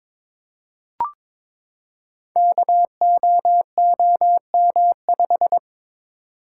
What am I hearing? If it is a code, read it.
Morse code, heard as a keyed tone, about 23 wpm, 690 Hz. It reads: KOOM5
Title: Code Groups - 5 Letter:Numbers - 22wpm